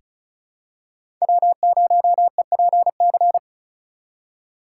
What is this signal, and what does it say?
Morse code, heard as a keyed tone, about 36 wpm, 700 Hz. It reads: W0EPC
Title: K1USN SST QSO Practice - 35wpm